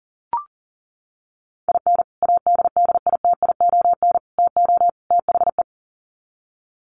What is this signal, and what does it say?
Morse code, heard as a keyed tone, about 40 wpm, 700 Hz. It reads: IN ADDITION TO THE
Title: Sets of 4 Words 40wpm